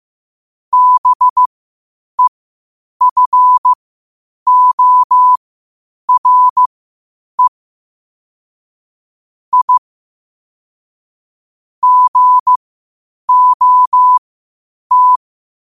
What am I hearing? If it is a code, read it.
Morse code, heard as a keyed tone, about 15 wpm, spaced as for 8 wpm, 1000 Hz. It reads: BEFORE I GOT
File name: qotdfiles / 8wpm